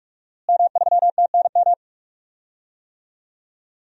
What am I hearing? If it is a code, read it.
Morse code, heard as a keyed tone, about 43 wpm, 700 Hz. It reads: M3TNM
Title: DXCC-100 Call Signs - 5 Characters - Repeated 2x - 45wpm